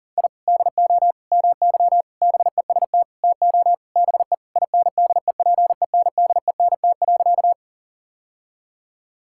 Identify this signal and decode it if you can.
Morse code, heard as a keyed tone, about 37 wpm, 700 Hz. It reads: I DO MY BEST TO BE INDEPENDENT.
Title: Sentences from Top 1000 Words - Review Part 6 - 40wpm